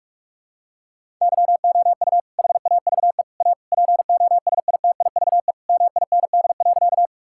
Morse code, heard as a keyed tone, about 41 wpm, 700 Hz. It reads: YOU HAVE A POSITIVE MIND.
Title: Sentences from Top 1000 Words - Review Part 5 - 45wpm